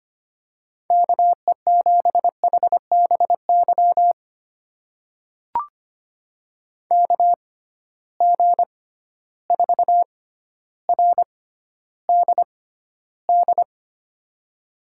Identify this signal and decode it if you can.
Morse code, heard as a keyed tone, about 25 wpm, spaced as for 13 wpm, 700 Hz. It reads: KE7HBY KG4RDD
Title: USA Call Signs - 6 Characters - 6x Character Spacing - 25wpm